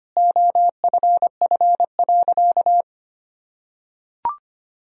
Morse code, heard as a keyed tone, about 25 wpm, 700 Hz. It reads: OFF.
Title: Sentences from Top 1000 Words - Review Part 1 - 25wpm